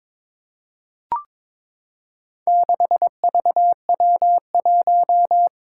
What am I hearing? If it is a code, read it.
Morse code, heard as a keyed tone, about 22 wpm, 700 Hz. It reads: 6VW1
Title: Four Challenging Letter-Number Combinations 22wpm